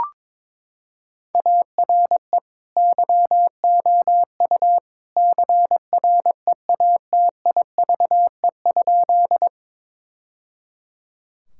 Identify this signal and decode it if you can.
Morse code, heard as a keyed tone, about 22 wpm, 700 Hz. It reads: ARE YOU CREATIVE?
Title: Sentences from Top 1000 Words - Review Part 5 - 22wpm